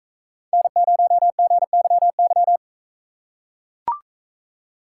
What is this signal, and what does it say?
Morse code, heard as a keyed tone, about 44 wpm, 700 Hz. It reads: N0GYY